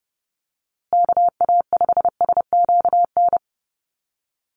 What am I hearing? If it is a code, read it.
Morse code, heard as a keyed tone, about 30 wpm, 700 Hz. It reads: KA5SQN
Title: USA Call Signs Repeated 3x 30wpm